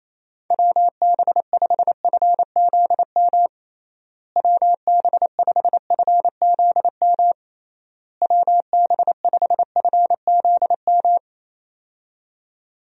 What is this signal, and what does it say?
Morse code, heard as a keyed tone, about 28 wpm, 700 Hz. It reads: WB5FZM WB5FZM WB5FZM
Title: USA Call Signs - 6 Characters - Repeated 3x - 28wpm